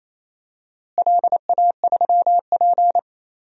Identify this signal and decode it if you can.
Morse code, heard as a keyed tone, about 28 wpm, 700 Hz. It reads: LA3P
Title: DXCC-100 Call Signs - 4 Characters - Repeated 3x - 28wpm